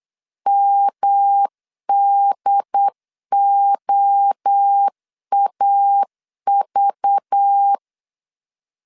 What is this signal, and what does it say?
Morse code, heard as a keyed tone, about 8 wpm, 800 Hz. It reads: MDOAV
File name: CW / DET / 1 6 1500C S B 8wpm 5142